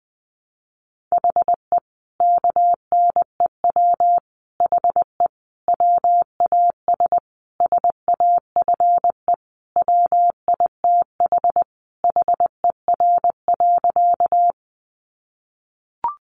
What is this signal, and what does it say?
Morse code, heard as a keyed tone, about 20 wpm, 700 Hz. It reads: HE KNEW HE WAS SAFE WITH HER.
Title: Sentences from Top 900 Words 20wpm